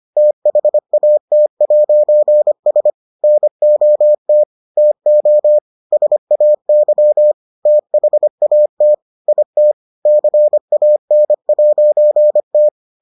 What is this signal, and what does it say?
Morse code, heard as a keyed tone, about 25 wpm, 600 Hz. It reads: THAT'S NOT TO SAY THAT IT CAN'T